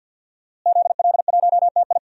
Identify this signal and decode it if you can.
Morse code, heard as a keyed tone, about 47 wpm, 700 Hz. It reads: ZL1TI
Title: DXCC-100 Call Signs - 5 Characters - Repeated 2x - 50wpm